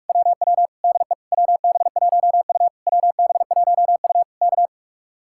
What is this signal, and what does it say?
Morse code, heard as a keyed tone, about 58 wpm, 700 Hz. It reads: WW DE WB1U WB1U K